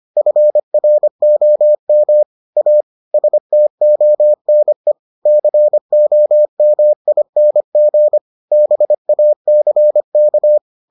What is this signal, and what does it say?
Morse code, heard as a keyed tone, about 25 wpm, 600 Hz. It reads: FROM A STONE COMING BACK